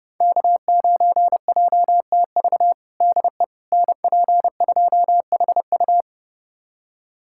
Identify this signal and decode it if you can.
Morse code, heard as a keyed tone, about 32 wpm, 700 Hz. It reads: K9JTV DE NP2HU
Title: USA Callsign DE Callsign - 5 Characters - 30wpm